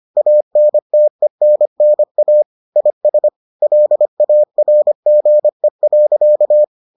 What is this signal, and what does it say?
Morse code, heard as a keyed tone, about 25 wpm, 600 Hz. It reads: ANTENNA IS LARGE.